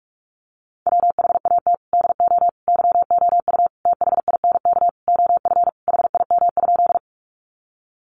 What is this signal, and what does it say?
Morse code, heard as a keyed tone, about 44 wpm, 700 Hz. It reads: WHAT DO YOU THINK OF HIM?
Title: Sentences from Top 200 Words 45wpm